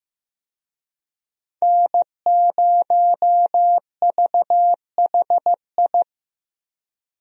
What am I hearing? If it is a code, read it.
Morse code, heard as a keyed tone, about 15 wpm, 700 Hz. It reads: N0VHI
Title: USA Call Signs - 5 Characters - Repeated 2x - 15wpm